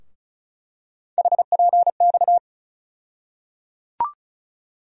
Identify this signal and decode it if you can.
Morse code, heard as a keyed tone, about 36 wpm, 700 Hz. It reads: HPX
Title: USA Call Sign Postfixes - 3 Letters 35wpm